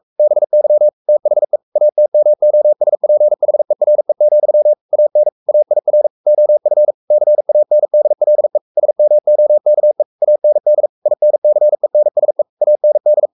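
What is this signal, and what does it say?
Morse code, heard as a keyed tone, about 44 wpm, 600 Hz. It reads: BY THE ATMOSPHERE, AN AIR OF CANDLE SMOKE AND INCENSE AND
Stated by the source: Morse code at 43 wpm sending Ken Follett's novel Never in English, from CW med Gnister.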